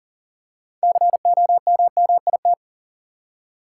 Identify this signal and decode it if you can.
Morse code, heard as a keyed tone, about 40 wpm, 700 Hz. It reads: COMMIT
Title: Six Letter Words - 40wpm